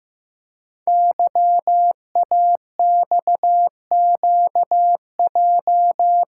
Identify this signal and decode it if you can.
Morse code, heard as a keyed tone, about 15 wpm, 690 Hz. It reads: YAXQJ
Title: Code Groups - 5 Letters - 15wpm